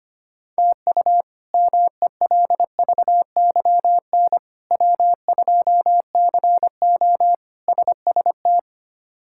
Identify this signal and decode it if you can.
Morse code, heard as a keyed tone, about 25 wpm, 700 Hz. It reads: TU MELVYN W2CO SST
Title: K1USN SST QSO Practice - 25wpm